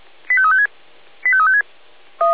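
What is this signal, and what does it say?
Five-tone selective call (ZVEI), heard as quick runs of five tones, tones of about 70 ms. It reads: 85356 85356
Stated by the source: ZVEI five-tone call received by SDR.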